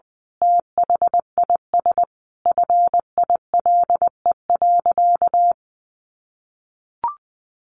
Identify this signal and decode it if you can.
Morse code, heard as a keyed tone, about 19 wpm, 700 Hz. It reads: THIS FILE.
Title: Sentences from Top 1200 Words 20wpm